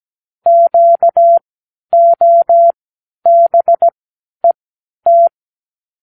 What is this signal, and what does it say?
Morse code, heard as a keyed tone, about 17 wpm, spaced as for 10 wpm, 680 Hz. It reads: QOBET
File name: k7qo / Lesson 36 (A-X)